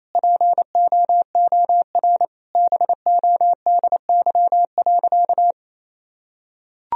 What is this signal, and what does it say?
Morse code, heard as a keyed tone, about 28 wpm, 700 Hz. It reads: POOR BODY.
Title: Sentences from Top 1000 Words - Review Part 2 - 28wpm